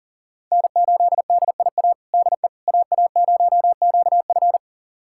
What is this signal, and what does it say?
Morse code, heard as a keyed tone, about 37 wpm, 700 Hz. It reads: N8DIA DE AA0QF
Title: USA Callsign DE Callsign - 5 Characters - 40wpm